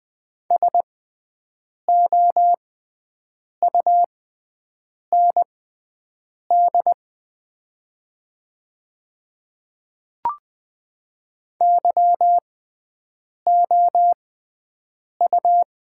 Morse code, heard as a keyed tone, about 20 wpm, spaced as for 7 wpm, 700 Hz. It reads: SOUND YOU
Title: 6x Character Spacing - Top 100 Words - 20wpm